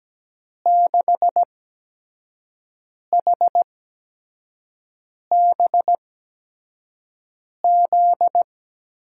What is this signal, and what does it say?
Morse code, heard as a keyed tone, about 17 wpm, spaced as for 5 wpm, 700 Hz. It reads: 6HBZ
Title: USA Call Signs - 6 Characters - 8x Character Spacing - 17wpm